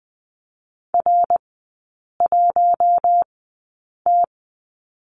Morse code, heard as a keyed tone, about 20 wpm, spaced as for 8 wpm, 700 Hz. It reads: R1T
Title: 152 Lesson 25e - Letter p - Three Characters 20wpm